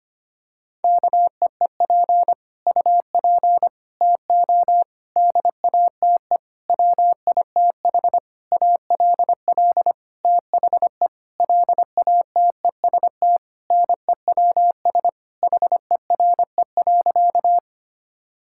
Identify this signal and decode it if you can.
Morse code, heard as a keyed tone, about 25 wpm, 700 Hz. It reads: KEEP UP TO DATE WITH ALL THE LATEST NEWS HERE.
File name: practice / 25wpm / Sentences with 9-Word Phrases - 25wpm